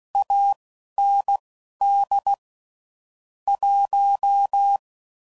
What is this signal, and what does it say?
Morse code, heard as a keyed tone, about 16 wpm, spaced as for 11 wpm, 800 Hz. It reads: AND 1
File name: CW Code Groups @10WPM / Technical Subjects / Gates Intro